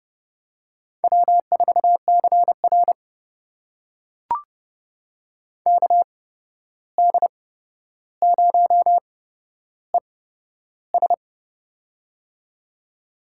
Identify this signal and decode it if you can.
Morse code, heard as a keyed tone, about 31 wpm, spaced as for 12 wpm, 700 Hz. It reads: W4CR KD0ES
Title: USA Call Signs - 8x Character Spacing - 30wpm